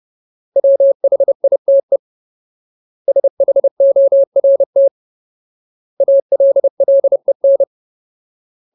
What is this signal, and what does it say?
Morse code, heard as a keyed tone, about 30 wpm, 550 Hz. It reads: WHITE SHORT ALLEN
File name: Ultimate Word List / 30WPM Ultimate